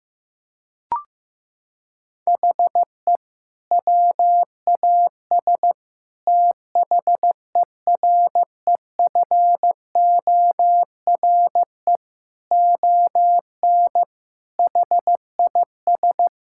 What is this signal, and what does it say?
Morse code, heard as a keyed tone, about 15 wpm, 700 Hz. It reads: HE WAS THEREFORE ON HIS
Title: Sentences from Top 1000 Words - Review Part 5 - 15wpm